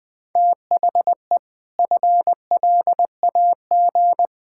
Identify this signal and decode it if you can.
Morse code, heard as a keyed tone, about 21 wpm, 700 Hz. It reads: THE FLAG
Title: Sentences with 4-Word Phrases - Spoken Beginning - 20wpm